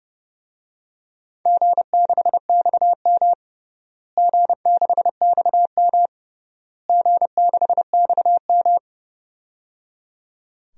Morse code, heard as a keyed tone, about 30 wpm, 700 Hz. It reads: G6XM G6XM G6XM
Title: DXCC-100 Call Signs - 4 Characters - Repeated 3x - 30wpm